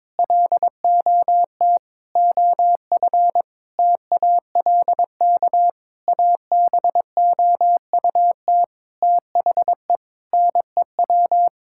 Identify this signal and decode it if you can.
Morse code, heard as a keyed tone, about 22 wpm, 700 Hz. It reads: LOT OF TALK ABOUT THE NEW
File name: practice / 22wpm / Sentences with 8-Word Phrases - 22wpm